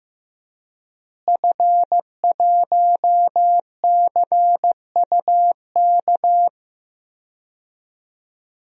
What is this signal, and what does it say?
Morse code, heard as a keyed tone, about 15 wpm, 700 Hz. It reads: F1CUK
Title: DXCC-100 Call Signs - 5 Characters - Repeated 3x - 15wpm